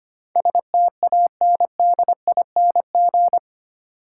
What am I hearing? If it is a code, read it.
Morse code, heard as a keyed tone, about 25 wpm, 690 Hz. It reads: STANDING
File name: 8 Letter Words - First 3 Letters Given - 25wpm